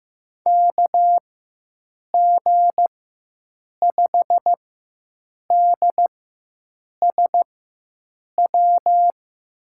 Morse code, heard as a keyed tone, about 15 wpm, spaced as for 7 wpm, 700 Hz. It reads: KG5DSW
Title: USA Call Signs - 4x Character Spacing - 15wpm